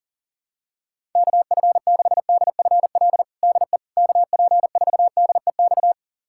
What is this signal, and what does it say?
Morse code, heard as a keyed tone, about 40 wpm, 700 Hz. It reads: KF6DFL DE KP4DEX
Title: USA Callsign DE Callsign - 6 Characters - 40wpm